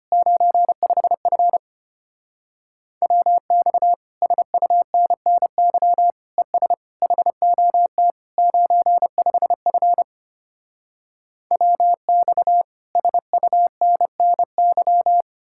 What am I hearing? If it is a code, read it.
Morse code, heard as a keyed tone, about 33 wpm, 700 Hz. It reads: 95F WX SUNNY ES HOT 95F WX SUNNY